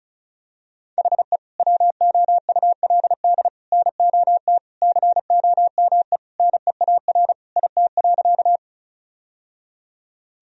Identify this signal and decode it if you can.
Morse code, heard as a keyed tone, about 36 wpm, 700 Hz. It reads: HE WOULD NOT COME NEAR IT.
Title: Sentences from Top 1000 Words - Review Part 1 - 35wpm